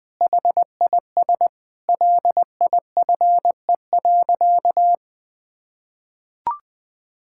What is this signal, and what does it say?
Morse code, heard as a keyed tone, about 21 wpm, 700 Hz. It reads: HIS LIFE.
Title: Sentences with 5-Word Phrases - Spoken Beginning - 20wpm